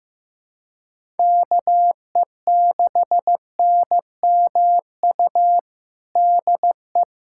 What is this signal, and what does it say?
Morse code, heard as a keyed tone, about 15 wpm, 700 Hz. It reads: KE6NMU DE
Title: USA Callsign DE Callsign 15wpm